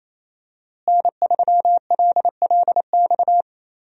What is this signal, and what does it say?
Morse code, heard as a keyed tone, about 27 wpm, 700 Hz. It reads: N3LLX
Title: USA Call Signs - 5 Characters - Repeated 3x - 28wpm